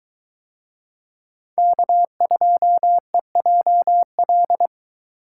Morse code, heard as a keyed tone, about 21 wpm, 700 Hz. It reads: K2EJL